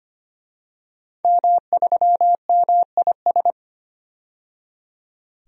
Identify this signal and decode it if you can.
Morse code, heard as a keyed tone, about 25 wpm, 700 Hz. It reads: M3MIS
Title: DXCC-100 Call Signs - 5 Characters - 25wpm